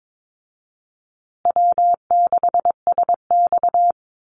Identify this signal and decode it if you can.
Morse code, heard as a keyed tone, about 22 wpm, 700 Hz. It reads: W6SX
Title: Easy Call Signs Repeated 2x - 22wpm